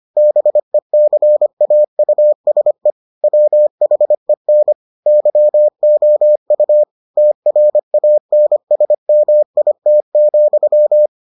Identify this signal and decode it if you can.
Morse code, heard as a keyed tone, about 25 wpm, 600 Hz. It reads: BECAUSE WHEN YOU TRANSMIT,